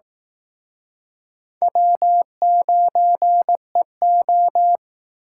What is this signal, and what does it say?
Morse code, heard as a keyed tone, about 18 wpm, 700 Hz. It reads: W9EO
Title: Speed Racing - USA 1x2 Call Signs 15wpm